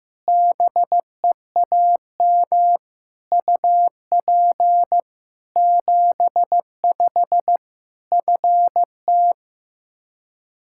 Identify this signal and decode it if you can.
Morse code, heard as a keyed tone, about 15 wpm, 700 Hz. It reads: BEAM UP 75 FT
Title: K1USN SST QSO Practice 15wpm